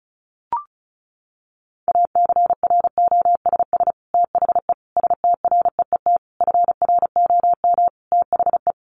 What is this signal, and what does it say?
Morse code, heard as a keyed tone, about 34 wpm, 700 Hz. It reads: ACROSS THE STREET FROM THE
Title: Sets of 5 Words Encore 35wpm